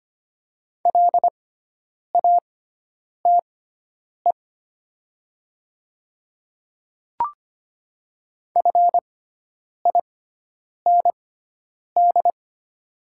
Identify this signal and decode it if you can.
Morse code, heard as a keyed tone, about 25 wpm, spaced as for 9 wpm, 700 Hz. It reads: LATE FIND
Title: 6x Character Spacing - Top 300 Words - 25wpm